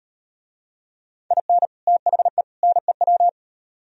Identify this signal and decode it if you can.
Morse code, heard as a keyed tone, about 38 wpm, 700 Hz. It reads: IN THE NEW